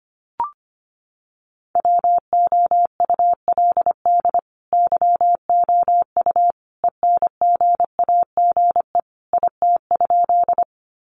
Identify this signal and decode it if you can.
Morse code, heard as a keyed tone, about 24 wpm, 700 Hz. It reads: WOULD YOU ENGAGE IT?
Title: Sentences from Top 1500 Words 25wpm